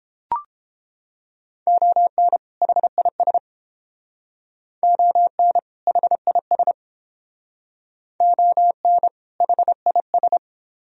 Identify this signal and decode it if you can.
Morse code, heard as a keyed tone, about 29 wpm, 700 Hz. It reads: ON HIS ON HIS ON HIS